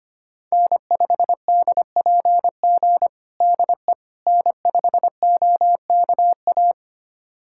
Morse code, heard as a keyed tone, about 25 wpm, 700 Hz. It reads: N5DPG DE N5OKA
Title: USA Callsign DE Callsign - 5 Characters - 25wpm